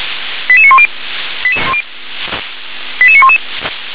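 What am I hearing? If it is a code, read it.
Five-tone selective call (ZVEI), heard as quick runs of five tones, tones of about 70 ms. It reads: 88910 88911
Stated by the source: ZVEI five-tone selective call received by SDR.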